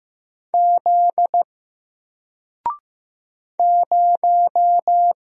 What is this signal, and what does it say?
Morse code, heard as a keyed tone, about 15 wpm, 700 Hz. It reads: Z 0
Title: Single Letter-Number Alt 15wpm